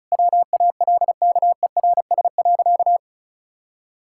Morse code, heard as a keyed tone, about 36 wpm, 700 Hz. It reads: WALKERS.
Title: Pangram Challenge v2 - 35wpm